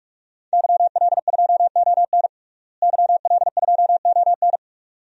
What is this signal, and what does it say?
Morse code, heard as a keyed tone, about 45 wpm, 700 Hz. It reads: YL2ON YL2ON
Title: DXCC-100 Call Signs - 5 Characters - Repeated 2x - 45wpm